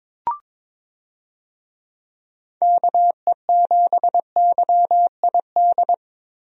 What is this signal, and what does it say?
Morse code, heard as a keyed tone, about 22 wpm, 700 Hz. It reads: KE7YID
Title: USA Call Signs - 6 Characters - Repeated 3x - 22wpm